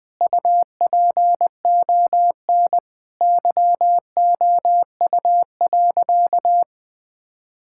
Morse code, heard as a keyed tone, about 20 wpm, 700 Hz. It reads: UPON YOU.
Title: Sentences from Top 1600 Words 20wpm